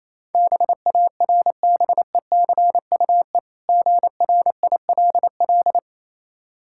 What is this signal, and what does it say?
Morse code, heard as a keyed tone, about 28 wpm, 700 Hz. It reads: BARBECUE GRILL